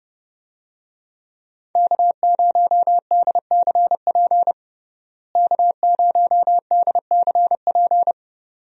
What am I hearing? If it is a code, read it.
Morse code, heard as a keyed tone, about 30 wpm, 700 Hz. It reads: K0DCP K0DCP
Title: USA Call Signs - 5 Characters - Repeated 2x - 30wpm